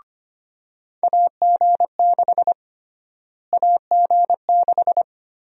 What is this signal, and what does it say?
Morse code, heard as a keyed tone, about 25 wpm, 700 Hz. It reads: AG6 AG6
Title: USA Call Sign Prefixes - Repeated 2x - 25wpm